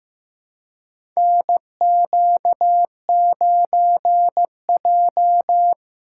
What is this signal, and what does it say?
Morse code, heard as a keyed tone, about 15 wpm, 700 Hz. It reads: NQ9J
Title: USA Call Signs - 2x Character Spacing - 15wpm